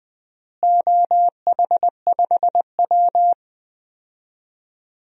Morse code, heard as a keyed tone, about 20 wpm, 700 Hz. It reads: OH5W
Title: DXCC-100 Call Signs - 4 Characters - Repeated 3x - 20wpm